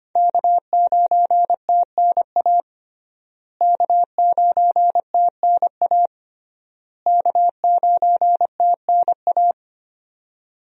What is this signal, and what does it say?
Morse code, heard as a keyed tone, about 25 wpm, 700 Hz. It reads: K9TNA K9TNA K9TNA
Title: USA Call Signs - 5 Characters - Repeated 3x - 25wpm